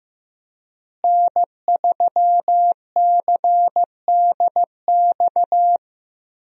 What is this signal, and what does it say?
Morse code, heard as a keyed tone, about 15 wpm, 700 Hz. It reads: N3CDX
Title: USA Call Signs - 5 Characters - Repeated 2x - 15wpm